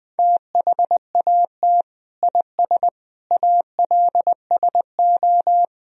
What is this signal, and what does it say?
Morse code, heard as a keyed tone, about 21 wpm, 700 Hz. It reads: THAT IS ALSO
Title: Sentences from Top 1000 Words - Review Part 1 - 20wpm